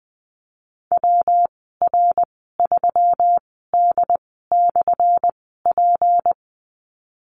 Morse code, heard as a keyed tone, about 20 wpm, spaced as for 14 wpm, 700 Hz. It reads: WR3D/P
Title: 179 Lesson 29g - Slash - Call Signs 20wpm